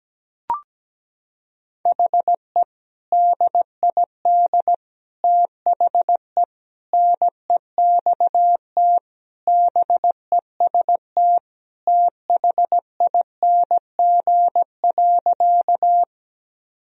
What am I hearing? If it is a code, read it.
Morse code, heard as a keyed tone, about 17 wpm, 700 Hz. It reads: HE DID THE NEXT BEST THING.
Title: Sentences from Top 1000 Words - Review Part 1 - 17wpm